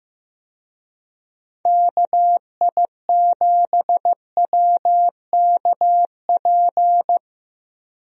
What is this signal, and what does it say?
Morse code, heard as a keyed tone, about 15 wpm, 700 Hz. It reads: KI7WKP
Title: USA Call Signs - 6 Characters - Repeated 2x - 15wpm